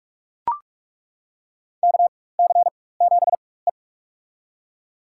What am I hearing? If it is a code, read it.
Morse code, heard as a keyed tone, about 42 wpm, spaced as for 21 wpm, 700 Hz. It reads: KC7E
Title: USA Call Signs - 4x Character Spacing - 45wpm